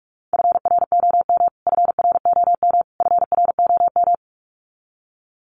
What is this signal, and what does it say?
Morse code, heard as a keyed tone, about 44 wpm, 700 Hz. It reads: FROM FROM FROM
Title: Top 500 Words Repeated 45wpm